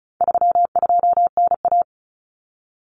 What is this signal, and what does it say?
Morse code, heard as a keyed tone, about 34 wpm, 700 Hz. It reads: 32NA
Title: Common Cut Numbers - 4-Digit 35wpm